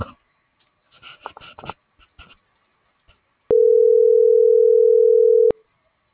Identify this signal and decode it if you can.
Ringback tone (North America), 440 + 480 Hz.